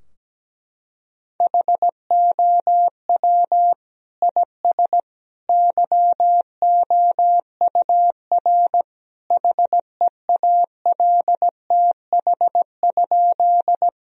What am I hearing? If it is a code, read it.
Morse code, heard as a keyed tone, about 17 wpm, 700 Hz. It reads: HOW IS YOUR HEALTH?
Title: Sentences from Top 1000 Words - Review Part 3 - 17wpm